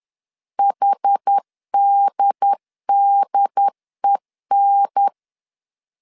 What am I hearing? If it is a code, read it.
Morse code, heard as a keyed tone, about 10 wpm, 800 Hz. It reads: HDDEN
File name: CW / DET / 1 3 1500C S C 10wpm 3335